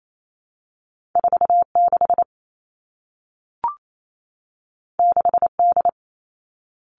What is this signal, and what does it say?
Morse code, heard as a keyed tone, about 28 wpm, 700 Hz. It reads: 4 6 6 D
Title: Two Challenging Letter-Number Combinations 28wpm